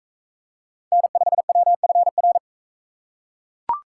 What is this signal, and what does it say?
Morse code, heard as a keyed tone, about 42 wpm, 700 Hz. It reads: N5WFR